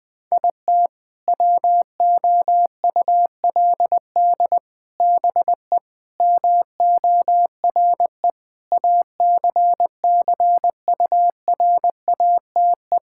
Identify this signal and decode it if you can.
Morse code, heard as a keyed tone, about 20 wpm, 700 Hz. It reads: IT WOULD BE MORE ACCURATE
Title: Sentences with 8-Word Phrases - Spoken Beginning - 20wpm